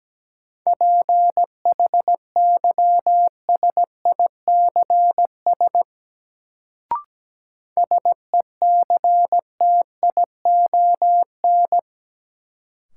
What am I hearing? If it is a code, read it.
Morse code, heard as a keyed tone, about 17 wpm, 700 Hz. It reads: PHYSICS SECTION
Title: Sentences from Top 1000 Words - Review Part 5 - New Word Drill - 17wpm